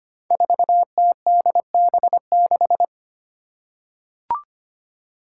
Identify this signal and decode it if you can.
Morse code, heard as a keyed tone, about 25 wpm, 700 Hz. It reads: 4TDB6
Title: Five Challenging Letter-Number Combinations 25wpm